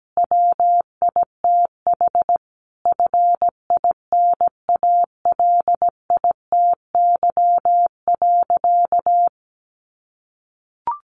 Morse code, heard as a keyed tone, about 17 wpm, 700 Hz. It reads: WITH FINALITY.
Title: A Sound Of Thunder 17wpm